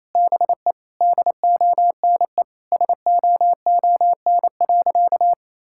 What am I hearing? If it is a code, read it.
Morse code, heard as a keyed tone, about 28 wpm, 700 Hz. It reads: BE DONE SOON.
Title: Sentences from Top 1600 Words 28wpm